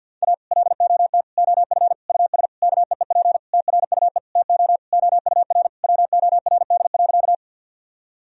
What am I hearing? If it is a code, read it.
Morse code, heard as a keyed tone, about 50 wpm, 700 Hz. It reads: A LOT OF US KEEP TRUE TO OUR WORD.
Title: Sentences from Top 1000 Words - Review Part 3 - 50wpm